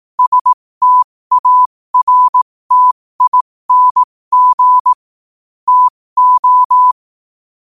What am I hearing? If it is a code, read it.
Morse code, heard as a keyed tone, about 18 wpm, spaced as for 15 wpm, 1000 Hz. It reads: STARTING TO
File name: qotdfiles / 15wpm